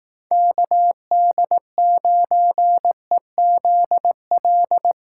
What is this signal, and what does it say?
Morse code, heard as a keyed tone, about 19 wpm, 700 Hz. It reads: KD9EZL
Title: Speed Racing - USA 2x3 Call Signs 15wpm